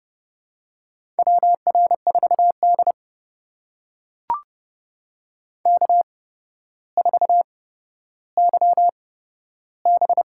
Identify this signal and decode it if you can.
Morse code, heard as a keyed tone, about 32 wpm, spaced as for 13 wpm, 700 Hz. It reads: WR4D K4YB
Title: USA Call Signs - 4 Characters - 8x Character Spacing - 30wpm